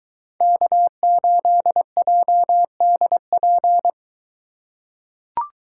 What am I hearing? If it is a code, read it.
Morse code, heard as a keyed tone, about 22 wpm, 700 Hz. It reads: K8JDP